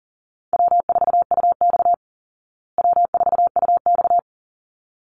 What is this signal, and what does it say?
Morse code, heard as a keyed tone, about 40 wpm, 700 Hz. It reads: W4UX W4UX
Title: Easy Call Signs Repeated 2x - 40wpm